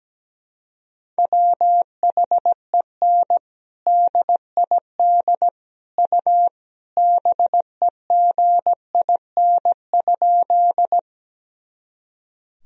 Morse code, heard as a keyed tone, about 17 wpm, 700 Hz. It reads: WHEN DID U BEGIN?
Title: Sentences from Top 300 Words and Abbreviations 17wpm